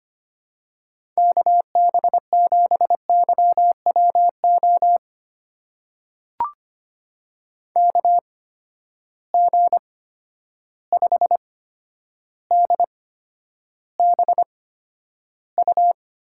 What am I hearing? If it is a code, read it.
Morse code, heard as a keyed tone, about 25 wpm, spaced as for 11 wpm, 700 Hz. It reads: KB7YWO KG5DBU
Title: USA Call Signs - 6 Characters - 8x Character Spacing - 25wpm